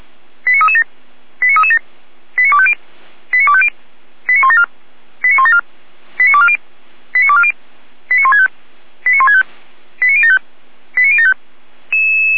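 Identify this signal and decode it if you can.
Five-tone selective call (ZVEI), heard as quick runs of five tones, tones of about 70 ms. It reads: 89337 89337 89250 89250 89174 89174 89240 89240 89165 89165 89075 89075
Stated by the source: ZVEI five-tone selective call received by SDR.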